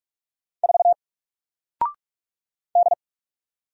Morse code, heard as a keyed tone, about 46 wpm, 700 Hz. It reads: 4 D
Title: Single Letter-Number Alt 45wpm